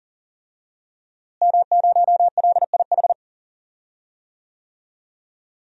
Morse code, heard as a keyed tone, about 40 wpm, 700 Hz. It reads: M0LIH
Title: DXCC-100 Call Signs - 5 Characters - Repeated 2x - 40wpm